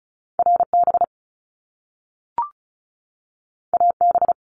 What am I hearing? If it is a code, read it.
Morse code, heard as a keyed tone, about 36 wpm, 700 Hz. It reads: RB AB
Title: Top Two Letter Combinations in Words 35wpm